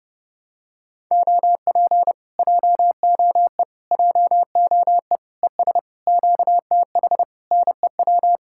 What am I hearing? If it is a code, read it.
Morse code, heard as a keyed tone, about 30 wpm, 700 Hz. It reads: OP JOE JOE ES QTH NEW